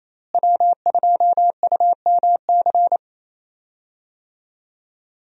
Morse code, heard as a keyed tone, about 28 wpm, 700 Hz. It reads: W2UMC
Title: USA Call Signs - 5 Characters - Repeated 3x - 28wpm